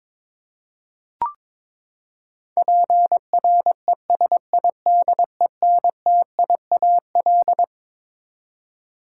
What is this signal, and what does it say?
Morse code, heard as a keyed tone, about 22 wpm, 700 Hz. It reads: PRESIDENTIAL
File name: Twelve Letter Words - 22wpm